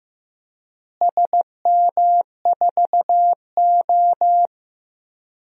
Morse code, heard as a keyed tone, about 15 wpm, 700 Hz. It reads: SM4O
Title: DXCC-100 Call Signs - 4 Characters - Repeated 3x - 15wpm